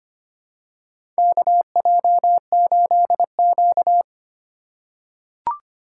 Morse code, heard as a keyed tone, about 25 wpm, 700 Hz. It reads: KJ8Q
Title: USA Call Signs - 4 Characters - 2x Character Spacing - 25wpm